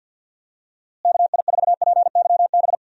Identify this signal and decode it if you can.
Morse code, heard as a keyed tone, about 52 wpm, 700 Hz. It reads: KI4PYB
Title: USA Call Signs - 6 Characters - Repeated 3x - 50wpm